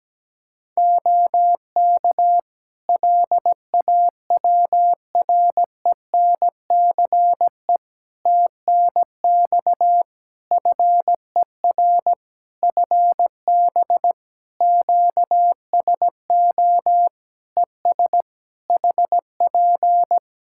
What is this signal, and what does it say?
Morse code, heard as a keyed tone, about 17 wpm, 700 Hz. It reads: OK LAWRENCE TNX FER FB QSO ES HP
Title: Practice QSO - Exchange 4 - Ending - 17wpm